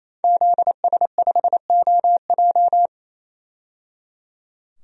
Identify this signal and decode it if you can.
Morse code, heard as a keyed tone, about 28 wpm, 700 Hz. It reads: ZS5OJ
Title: DXCC-100 Call Signs - 5 Characters - 28wpm